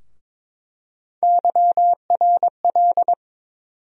Morse code, heard as a keyed tone, about 22 wpm, 700 Hz. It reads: YRL
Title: USA Call Sign Postfixes - 3 Letters 22wpm